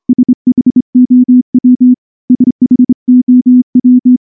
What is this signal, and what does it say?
Morse code, heard as a keyed tone, about 25 wpm, 250 Hz. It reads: SHOW SHOW
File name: morse / morse-training-25WPM